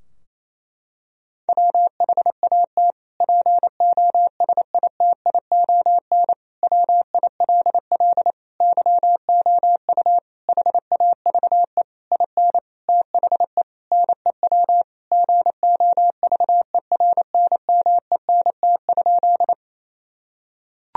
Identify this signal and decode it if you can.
Morse code, heard as a keyed tone, about 28 wpm, 700 Hz. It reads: WHAT POSITION WILL YOU HAVE IN THE NEW GOVERNMENT?
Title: Sentences from Top 1000 Words - Review Part 3 - 28wpm